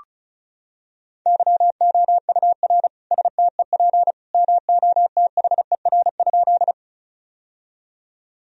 Morse code, heard as a keyed tone, about 36 wpm, 700 Hz. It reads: YOUR STEP MOTHER?
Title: Sentences from Top 1000 Words - Review Part 2 - 35wpm